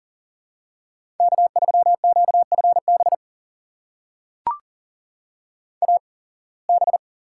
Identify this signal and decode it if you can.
Morse code, heard as a keyed tone, about 40 wpm, spaced as for 26 wpm, 700 Hz. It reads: K3QFB AB
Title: USA Call Signs - 5 Characters - 8x Character Spacing - 40wpm